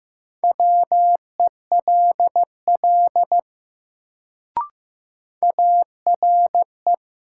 Morse code, heard as a keyed tone, about 15 wpm, 700 Hz. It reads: WELL ARE
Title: Sets of 2 Words - New Word Drill - 15wpm